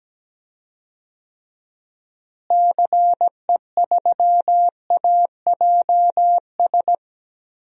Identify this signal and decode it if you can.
Morse code, heard as a keyed tone, about 17 wpm, 700 Hz. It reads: CE3AJS
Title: DXCC-100 Call Signs - 6 Characters - Repeated 3x - 17wpm